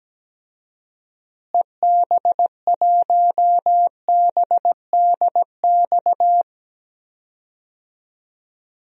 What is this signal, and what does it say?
Morse code, heard as a keyed tone, about 17 wpm, 700 Hz. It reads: EB1BDX
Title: DXCC-100 Call Signs - 6 Characters - Repeated 2x - 17wpm